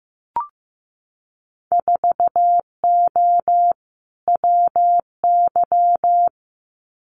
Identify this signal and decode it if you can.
Morse code, heard as a keyed tone, about 15 wpm, 700 Hz. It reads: 4O WY
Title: Winter Field Day Exchanges Encore 15wpm